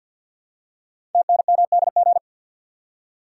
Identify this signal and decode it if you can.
Morse code, heard as a keyed tone, about 48 wpm, 700 Hz. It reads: TNMDG
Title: Code Groups - 5 Letters - 50wpm